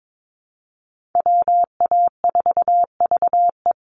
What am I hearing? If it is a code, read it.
Morse code, heard as a keyed tone, about 22 wpm, 700 Hz. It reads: WA4VE
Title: USA Call Signs 22wpm